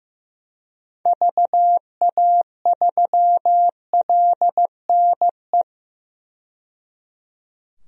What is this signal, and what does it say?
Morse code, heard as a keyed tone, about 15 wpm, 700 Hz. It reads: VA3LNE
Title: DXCC-100 Call Signs - 6 Characters - 15wpm